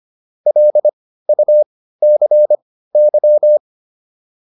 Morse code, heard as a keyed tone, about 25 wpm, spaced as for 15 wpm, 600 Hz. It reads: LUCY